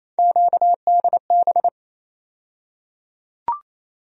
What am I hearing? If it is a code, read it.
Morse code, heard as a keyed tone, about 28 wpm, 700 Hz. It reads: QDB
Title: USA Call Sign Postfixes - 3 Letters 28wpm